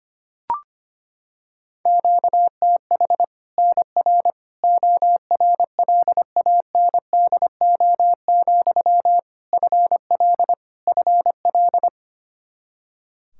Morse code, heard as a keyed tone, about 25 wpm, 700 Hz. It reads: QTH NR ORLANDO, FL FL
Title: Practice QSO - Exchange 1 - Beginning - 25wpm